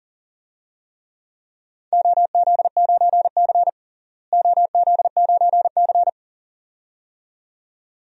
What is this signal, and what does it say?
Morse code, heard as a keyed tone, about 40 wpm, 700 Hz. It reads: OZ9C OZ9C
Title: DXCC-100 Call Signs - 4 Characters - Repeated 2x - 40wpm